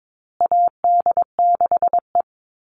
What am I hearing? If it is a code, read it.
Morse code, heard as a keyed tone, about 22 wpm, 700 Hz. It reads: AD6E
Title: Call Signs Repeated 2x - 22wpm